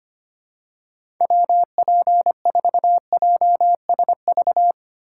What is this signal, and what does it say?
Morse code, heard as a keyed tone, about 25 wpm, 700 Hz. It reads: WP4JSV
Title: USA Call Signs - 6 Characters - 4x Character Spacing - 25wpm